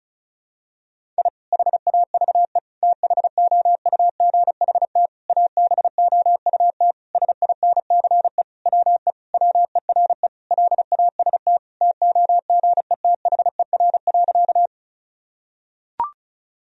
Morse code, heard as a keyed tone, about 36 wpm, 700 Hz. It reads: I HAVE THOUGHT ABOUT SINCE WE WERE LAST TOGETHER.
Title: Sentences from Top 1000 Words - Review Part 1 - 35wpm